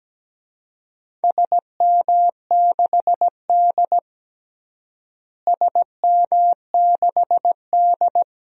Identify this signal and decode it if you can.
Morse code, heard as a keyed tone, about 17 wpm, 700 Hz. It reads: SM6D SM6D
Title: DXCC-100 Call Signs - 4 Characters - Repeated 2x - 17wpm